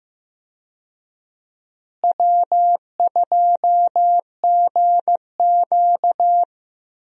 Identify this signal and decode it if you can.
Morse code, heard as a keyed tone, about 15 wpm, 700 Hz. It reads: W2GQ
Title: USA Call Signs - 4 Characters - Repeated 3x - 15wpm